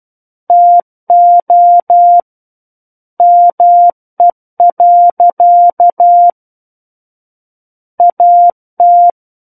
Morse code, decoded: TO ME. AT

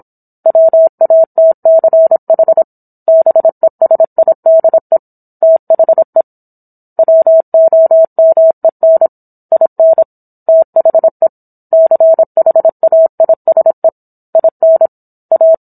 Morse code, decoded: WATCH BESIDE THE WOMEN IN THE CHAISE IN A

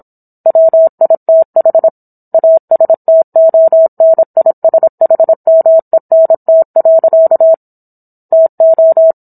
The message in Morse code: WITH ASTONISHMENT. TO

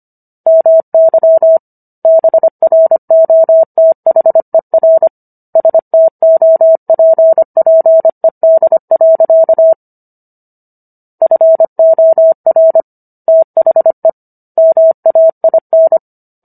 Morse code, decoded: MY BROTHER STOPPED. FOR THE MAIN

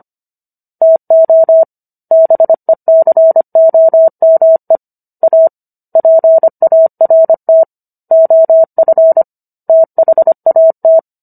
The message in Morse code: TO BECOME A PART OF THAT

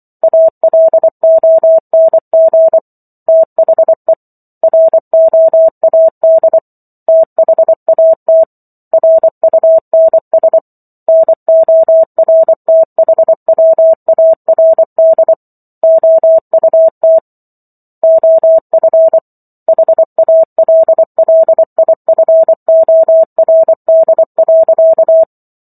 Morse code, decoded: ALONG THE ROAD THAT RUNS NORTHWARD OUT OF HALLIFORD.

